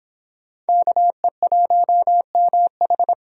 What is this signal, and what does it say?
Morse code, heard as a keyed tone, about 26 wpm, 700 Hz. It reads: KE1MH